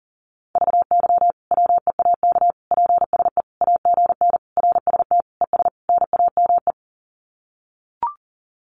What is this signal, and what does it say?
Morse code, heard as a keyed tone, about 40 wpm, 700 Hz. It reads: VY WEAK PSE AGN RST ES NAME
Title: Common QSO Phrases 40wpm